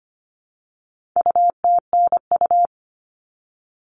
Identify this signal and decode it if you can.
Morse code, heard as a keyed tone, about 25 wpm, 700 Hz. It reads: UTNU
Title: All Cut Numbers - 4-Digit 25wpm